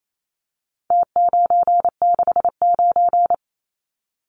Morse code, heard as a keyed tone, about 28 wpm, 700 Hz. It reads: T969
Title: Numbers - Contest Style - 4-Digit 28wpm